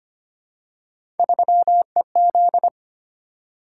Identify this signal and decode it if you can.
Morse code, heard as a keyed tone, about 25 wpm, 700 Hz. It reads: 3EZ
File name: Three Letter-Number Combinations - 25wpm